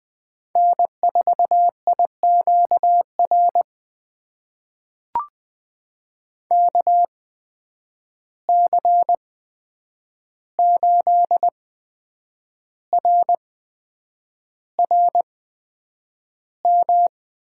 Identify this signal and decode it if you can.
Morse code, heard as a keyed tone, about 20 wpm, spaced as for 8 wpm, 700 Hz. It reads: N4IQR KC8RRM
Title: USA Call Signs - 8x Character Spacing - 20wpm